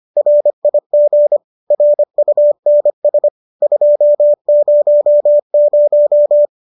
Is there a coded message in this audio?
Yes: Morse code from 0.2 to 6.5 s, heard as a keyed tone, about 25 wpm, 590 Hz. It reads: RIG RUNS 200